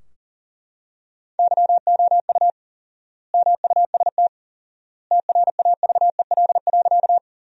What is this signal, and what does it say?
Morse code, heard as a keyed tone, about 40 wpm, 700 Hz. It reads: YOU MUST TRAVEL.
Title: 4x Word-Spacing - Sentences from Top 400 Words - 40wpm